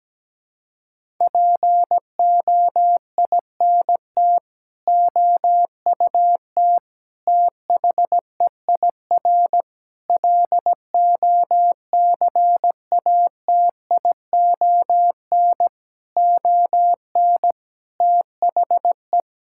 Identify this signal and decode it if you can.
Morse code, heard as a keyed tone, about 17 wpm, 700 Hz. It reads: POINT OUT THEIR LOCATION ON THE
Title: Sentences from Top 1000 Words - Review Part 6 - 17wpm